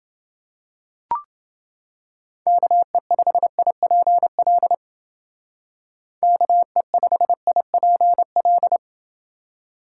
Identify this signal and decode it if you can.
Morse code, heard as a keyed tone, about 29 wpm, 700 Hz. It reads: KE5IPL KE5IPL